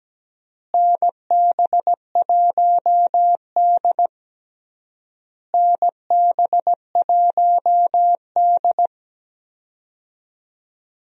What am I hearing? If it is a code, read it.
Morse code, heard as a keyed tone, about 17 wpm, 700 Hz. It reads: NB1D NB1D
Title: USA Call Signs - 4 Characters - Repeated 3x - 17wpm